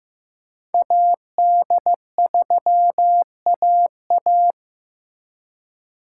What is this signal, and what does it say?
Morse code, heard as a keyed tone, about 18 wpm, 700 Hz. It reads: AD3AA
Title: Speed Racing - USA 2x2 Call Signs 15wpm